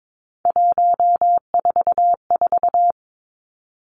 Morse code, heard as a keyed tone, about 22 wpm, 700 Hz. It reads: 144
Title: Three Challenging Letter-Number Combinations 22wpm